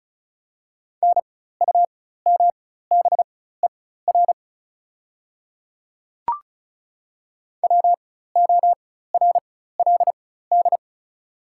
Morse code, heard as a keyed tone, about 36 wpm, spaced as for 16 wpm, 690 Hz. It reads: NUMBER WORLD
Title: 4x Character Spacing - Top 200 Words - 35wpm